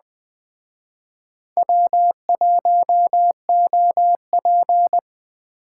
Morse code, heard as a keyed tone, about 20 wpm, 700 Hz. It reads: W1OP